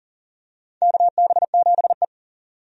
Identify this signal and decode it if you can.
Morse code, heard as a keyed tone, about 40 wpm, 700 Hz. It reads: KB7E
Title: USA Call Signs - 4 Characters - 40wpm